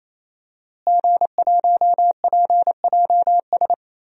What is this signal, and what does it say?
Morse code, heard as a keyed tone, about 27 wpm, 700 Hz. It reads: G1PJS
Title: DXCC-100 Call Signs - 5 Characters - Repeated 3x - 28wpm